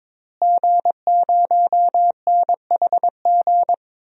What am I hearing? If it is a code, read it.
Morse code, heard as a keyed tone, about 22 wpm, 700 Hz. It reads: G0NHG
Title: DXCC-100 Call Signs - 5 Characters - Repeated 2x - 22wpm